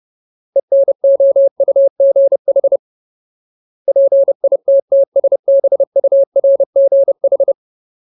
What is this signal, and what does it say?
Morse code, heard as a keyed tone, about 30 wpm, 550 Hz. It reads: ENOUGH PITTSBURGH
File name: Ultimate Word List / 30WPM Ultimate